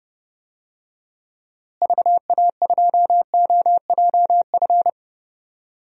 Morse code, heard as a keyed tone, about 32 wpm, 700 Hz. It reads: VA2OJF
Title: DXCC-100 Call Signs - 6 Characters - Repeated 3x - 30wpm